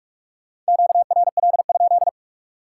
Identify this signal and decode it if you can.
Morse code, heard as a keyed tone, about 58 wpm, 700 Hz. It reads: QRL?